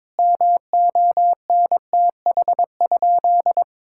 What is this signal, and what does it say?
Morse code, heard as a keyed tone, about 22 wpm, 700 Hz. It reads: MONTH?
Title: Sentences from Top 1600 Words 22wpm